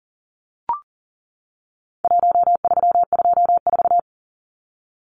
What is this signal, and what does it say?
Morse code, heard as a keyed tone, about 40 wpm, 700 Hz. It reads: 1324
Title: Numbers - Contest Style - 4-Digit 40wpm